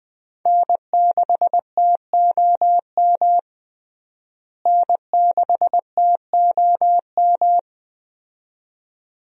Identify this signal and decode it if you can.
Morse code, heard as a keyed tone, about 20 wpm, 700 Hz. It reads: N6TOM N6TOM
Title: USA Call Signs - 5 Characters - Repeated 2x - 20wpm